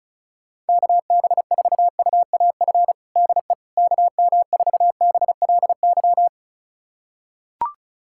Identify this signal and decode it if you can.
Morse code, heard as a keyed tone, about 36 wpm, 700 Hz. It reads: KB4UAF DE KM4BLY
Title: USA Callsign DE Callsign 35wpm